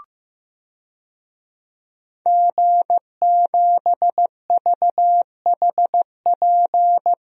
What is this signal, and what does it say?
Morse code, heard as a keyed tone, about 15 wpm, 700 Hz. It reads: G7VHP
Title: DXCC-100 Call Signs - 5 Characters - Repeated 2x - 15wpm